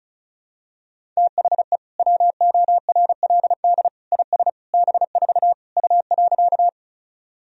Morse code, heard as a keyed tone, about 34 wpm, 700 Hz. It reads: THE WORLD IS B4 U.
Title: Sentences from Top 200 Words and Abbreviations 35wpm